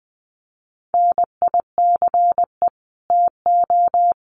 Morse code, heard as a keyed tone, about 20 wpm, 700 Hz. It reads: NICE TO
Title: Common QSO Phrases 20wpm